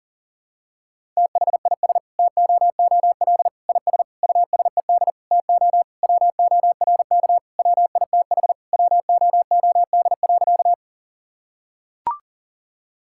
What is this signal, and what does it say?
Morse code, heard as a keyed tone, about 40 wpm, 700 Hz. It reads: THIS TOOL IS USED TO WORK WITH WOOD.
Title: Sentences from Top 1000 Words - Review Part 6 - 40wpm